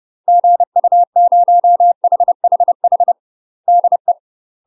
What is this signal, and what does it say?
Morse code, heard as a keyed tone, about 30 wpm, 700 Hz. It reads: GU0HHH DE
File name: UK MASTER QSOs TRAINING / 30WPM / UK MASTER QSOs 0023